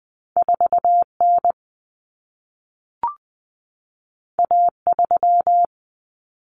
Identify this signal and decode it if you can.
Morse code, heard as a keyed tone, about 20 wpm, 700 Hz. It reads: 4N A3
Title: Common Cut Numbers - 2-Digit 20wpm